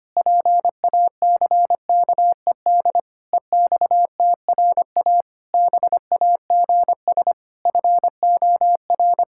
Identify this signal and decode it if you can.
Morse code, heard as a keyed tone, about 25 wpm, 700 Hz. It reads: PACKED EXTRA BAGS FOR
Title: Pangram Challenge v2 - 25wpm